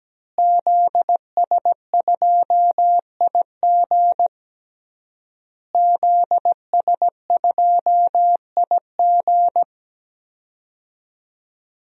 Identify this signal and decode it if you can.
Morse code, heard as a keyed tone, about 17 wpm, 700 Hz. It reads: ZS2IG ZS2IG
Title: DXCC-100 Call Signs - 5 Characters - Repeated 3x - 17wpm